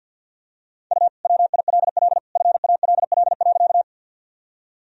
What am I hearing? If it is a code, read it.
Morse code, heard as a keyed tone, about 52 wpm, 700 Hz. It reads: U WILL FALL.
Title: Sentences from Top 500 Words and Abbreviations 50wpm